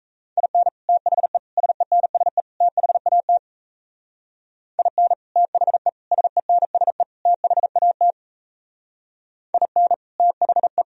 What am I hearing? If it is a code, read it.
Morse code, heard as a keyed tone, about 34 wpm, 700 Hz. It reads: IN THE SENSE THAT IN THE SENSE THAT IN THE
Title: Speed-Racing - Sets of 4 Words 28wpm